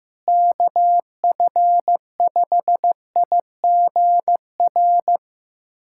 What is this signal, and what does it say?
Morse code, heard as a keyed tone, about 15 wpm, 700 Hz. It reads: KF5IGR
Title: USA Callsign DE Callsign - 6 Characters - 15wpm